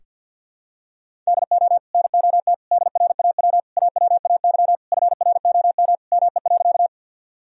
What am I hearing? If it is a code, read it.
Morse code, heard as a keyed tone, about 53 wpm, 700 Hz. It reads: DO NOT DRAW AWAY FROM ME.
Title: Sentences from Top 1000 Words - Review Part 1 - 50wpm